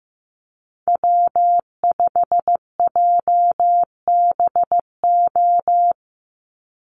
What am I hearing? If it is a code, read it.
Morse code, heard as a keyed tone, about 15 wpm, 700 Hz. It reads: W5JBO
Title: Call Signs Repeated 3x - 15wpm